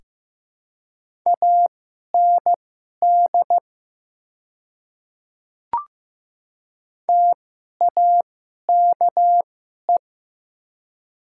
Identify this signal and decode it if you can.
Morse code, heard as a keyed tone, about 15 wpm, spaced as for 11 wpm, 700 Hz. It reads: AND TAKE
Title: 2x Character Spacing - Top 300 Words - 15wpm